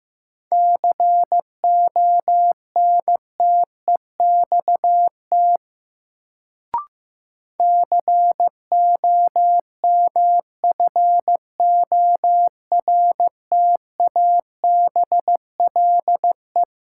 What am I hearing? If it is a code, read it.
Morse code, heard as a keyed tone, about 15 wpm, 700 Hz. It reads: CONTEXT COMFORTABLE
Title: Sentences from Top 1000 Words - Review Part 7 - New Word Drill - 15wpm